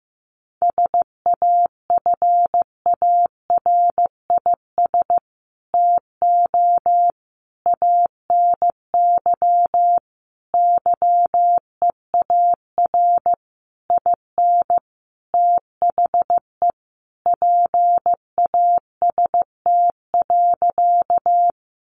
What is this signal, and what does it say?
Morse code, heard as a keyed tone, about 15 wpm, 700 Hz. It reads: SAFARIS TO ANY YEAR IN THE PAST.
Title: A Sound Of Thunder 15wpm